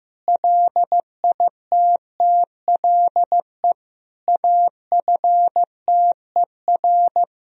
Morse code, heard as a keyed tone, about 15 wpm, 700 Hz. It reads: LITTLE AFTER
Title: Sentences from Top 1000 Words - Review Part 3 - 15wpm